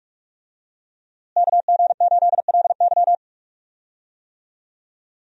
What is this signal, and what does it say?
Morse code, heard as a keyed tone, about 41 wpm, 700 Hz. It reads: KG8LY
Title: USA Call Signs - 5 Characters - Repeated 3x - 45wpm